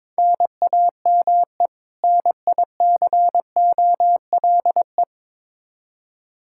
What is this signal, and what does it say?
Morse code, heard as a keyed tone, about 22 wpm, 700 Hz. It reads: NAME NICOLE
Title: Practice QSO - Exchange 1 - Beginning - 22wpm